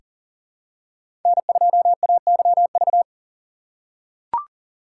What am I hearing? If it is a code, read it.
Morse code, heard as a keyed tone, about 40 wpm, 700 Hz. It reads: N2AYV
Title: USA Call Signs - 4x Character Spacing - 40wpm